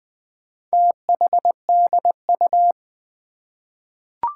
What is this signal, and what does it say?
Morse code, heard as a keyed tone, about 20 wpm, 700 Hz. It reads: THDU
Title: Four Challenging Letter-Number Combinations 20wpm